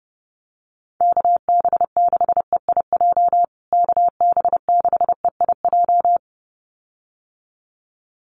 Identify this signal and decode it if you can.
Morse code, heard as a keyed tone, about 30 wpm, 700 Hz. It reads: KB6EIJ KB6EIJ
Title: USA Call Signs Repeated 2x 30wpm